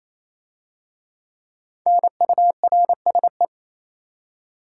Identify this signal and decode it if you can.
Morse code, heard as a keyed tone, about 27 wpm, 690 Hz. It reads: NURSE
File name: Easy Words - Speed-Racing with Hints - 22wpm